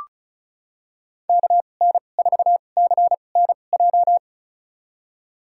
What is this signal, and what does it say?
Morse code, heard as a keyed tone, about 34 wpm, spaced as for 25 wpm, 700 Hz. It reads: KN4CNJ
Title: USA Call Signs - 6 Characters - 2x Character Spacing - 35wpm